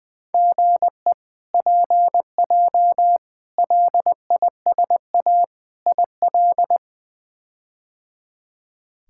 Morse code, heard as a keyed tone, about 20 wpm, 700 Hz. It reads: GE PJ LISA IL
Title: K1USN SST QSO Practice - 20wpm